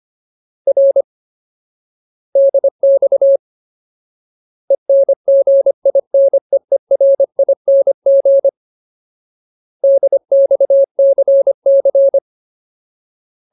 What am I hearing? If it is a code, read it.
Morse code, heard as a keyed tone, about 25 wpm, 550 Hz. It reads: R DX ENGINEERING DXCC
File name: Ultimate Word List / 25WPM Ultimate